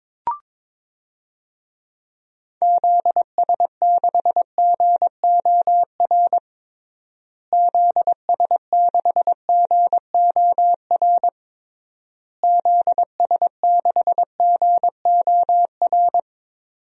Morse code, heard as a keyed tone, about 22 wpm, 700 Hz. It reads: ZS6GOR ZS6GOR ZS6GOR
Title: DXCC-100 Call Signs - 6 Characters - Repeated 3x - 22wpm